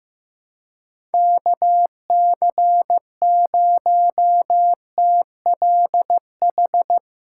Morse code, heard as a keyed tone, about 15 wpm, 700 Hz. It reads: KC0TLH
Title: USA Call Signs - 4x Character Spacing - 15wpm